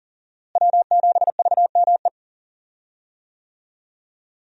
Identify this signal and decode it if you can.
Morse code, heard as a keyed tone, about 40 wpm, 700 Hz. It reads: W7VME
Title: USA Call Signs - 5 Characters - Repeated 3x - 40wpm